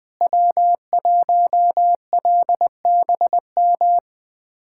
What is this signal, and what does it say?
Morse code, heard as a keyed tone, about 20 wpm, 700 Hz. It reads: W1LBM
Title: USA Callsign DE Callsign - 5 Characters - 20wpm